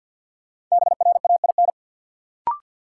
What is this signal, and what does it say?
Morse code, heard as a keyed tone, about 53 wpm, 700 Hz. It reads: BRAIN
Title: Sentences from Top 1000 Words - Review Part 6 - New Word Drill - 50wpm